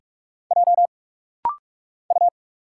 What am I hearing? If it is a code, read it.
Morse code, heard as a keyed tone, about 46 wpm, 700 Hz. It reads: J U
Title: Single Letter-Number Alt - Rapid-Fire 45wpm